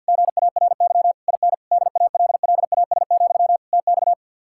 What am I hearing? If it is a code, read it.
Morse code, heard as a keyed tone, about 67 wpm, spaced as for 58 wpm, 700 Hz. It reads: GARY IN DALLAS, TX
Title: Speed-Racing Common QSO Phrases Encore - 50wpm